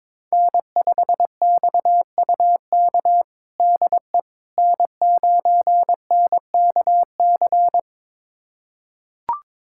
Morse code, heard as a keyed tone, about 22 wpm, 700 Hz. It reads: N5XUK DE N9NKC
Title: USA Callsign DE Callsign - 5 Characters - 22wpm